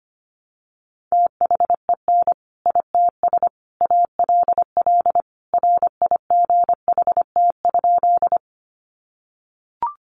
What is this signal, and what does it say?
Morse code, heard as a keyed tone, about 25 wpm, 700 Hz. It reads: THEN ITS ALL RIGHT?
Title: Sentences from Top 600 Words 25wpm